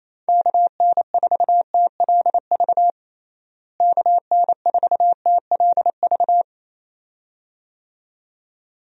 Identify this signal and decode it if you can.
Morse code, heard as a keyed tone, about 27 wpm, 700 Hz. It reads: KN4TLV KN4TLV
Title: USA Call Signs - 6 Characters - Repeated 2x - 28wpm